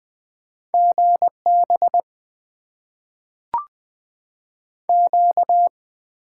Morse code, heard as a keyed tone, about 20 wpm, 700 Hz. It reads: GB Q
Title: Top 90 QSO Elements - 8x Character Spacing - 20wpm